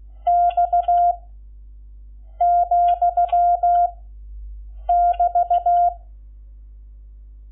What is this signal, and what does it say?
Morse code, heard as a keyed tone, about 16 wpm, 700 Hz. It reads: X , <BT>
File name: Lecciones / CLASE Nº11